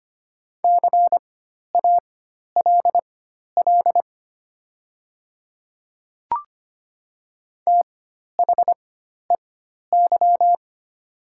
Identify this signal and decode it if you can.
Morse code, heard as a keyed tone, about 25 wpm, spaced as for 12 wpm, 690 Hz. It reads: CALL THEY
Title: 4x Character Spacing - Top 300 Words - 25wpm